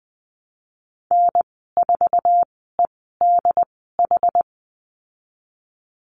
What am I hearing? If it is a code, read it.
Morse code, heard as a keyed tone, about 19 wpm, spaced as for 14 wpm, 700 Hz. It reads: N4EDH
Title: 076 Lesson 13g - Letter L - Call Signs 20wpm